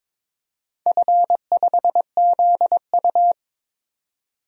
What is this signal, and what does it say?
Morse code, heard as a keyed tone, about 22 wpm, 700 Hz. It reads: F5ZU
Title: DXCC-100 Call Signs - 4 Characters - 22wpm